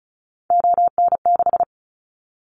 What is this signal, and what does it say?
Morse code, heard as a keyed tone, about 34 wpm, 700 Hz. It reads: ON6
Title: Callsign Prefixes for Top 100 DXCC - 35wpm